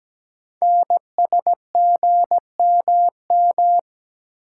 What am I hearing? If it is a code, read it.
Morse code, heard as a keyed tone, about 17 wpm, 700 Hz. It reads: NSGMM
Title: Code Groups - 5 Letters - 17wpm